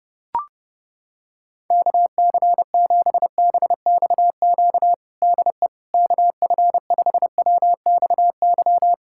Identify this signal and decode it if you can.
Morse code, heard as a keyed tone, about 30 wpm, 700 Hz. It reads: KC7BXQ DE KF5WXY
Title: USA Callsign DE Callsign 30wpm